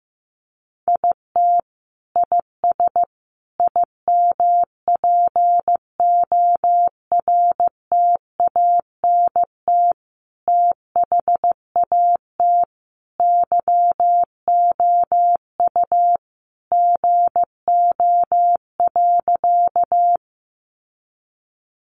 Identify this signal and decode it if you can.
Morse code, heard as a keyed tone, about 15 wpm, 700 Hz. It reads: IT IS IMPORTANT THAT YOU GO.
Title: Sentences from Top 600 Words 15wpm